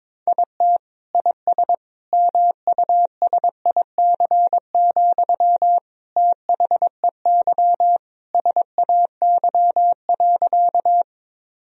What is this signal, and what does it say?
Morse code, heard as a keyed tone, about 21 wpm, 700 Hz. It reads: IT IS MUSIC, THEY SAY.
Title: Sentences from Top 1000 Words - Review Part 1 - 22wpm